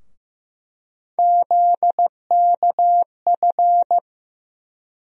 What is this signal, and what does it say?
Morse code, heard as a keyed tone, about 15 wpm, 700 Hz. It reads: ZKF
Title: USA Call Sign Postfixes - 3 Letters 15wpm